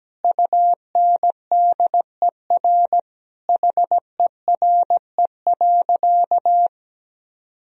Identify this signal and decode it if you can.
Morse code, heard as a keyed tone, about 17 wpm, 700 Hz. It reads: UNDER HERE.
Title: Sentences from Top 1000 Words - Review Part 1 - 17wpm